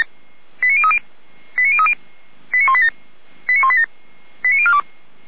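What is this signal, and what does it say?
Five-tone selective call (ZVEI), heard as quick runs of five tones, tones of about 70 ms. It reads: 89030 89030 89187 89187 89042
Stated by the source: ZVEI five-tone selective call received on an SDR.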